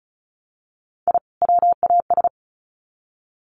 Morse code, heard as a keyed tone, about 34 wpm, 700 Hz. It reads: I WAS
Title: Sets of 2 Words 35wpm